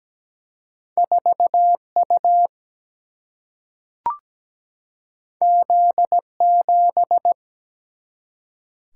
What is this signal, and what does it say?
Morse code, heard as a keyed tone, about 17 wpm, 700 Hz. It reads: 4U Z7
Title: Single-Letter Single-Number 17wpm